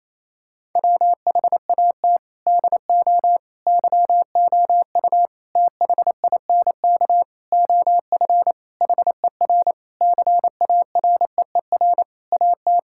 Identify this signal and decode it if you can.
Morse code, heard as a keyed tone, about 28 wpm, 700 Hz. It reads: WHAT DO YOU THINK OF HER CAREER AT
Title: Sentences from Top 1000 Words - Review Part 6 - 28wpm